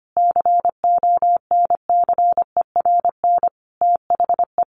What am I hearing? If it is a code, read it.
Morse code, heard as a keyed tone, about 25 wpm, 700 Hz. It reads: CONCERN THE
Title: Sentences from Top 1500 Words 25wpm